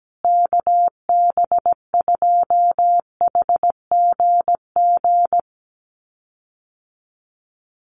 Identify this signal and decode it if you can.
Morse code, heard as a keyed tone, about 17 wpm, 700 Hz. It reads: KB2HGG
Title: USA Call Signs Repeated 2x 17wpm